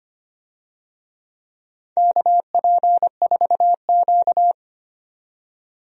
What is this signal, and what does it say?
Morse code, heard as a keyed tone, about 25 wpm, 690 Hz. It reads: KP4Q
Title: DXCC-100 Call Signs - 4 Characters - 25wpm